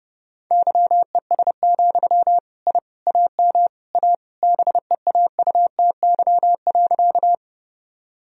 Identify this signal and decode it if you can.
Morse code, heard as a keyed tone, about 32 wpm, 700 Hz. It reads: YES, I AM A BEAUTY.
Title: Sentences from Top 1000 Words - Review Part 2 - 30wpm